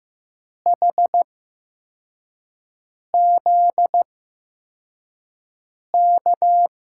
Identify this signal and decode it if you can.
Morse code, heard as a keyed tone, about 15 wpm, spaced as for 4 wpm, 700 Hz. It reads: HZK
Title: USA Call Signs - 5 Characters - 8x Character Spacing - 15wpm